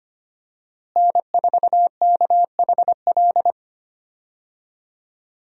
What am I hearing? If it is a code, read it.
Morse code, heard as a keyed tone, about 25 wpm, 700 Hz. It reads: N4KHL
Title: USA Call Signs - 5 Characters - Repeated 2x - 25wpm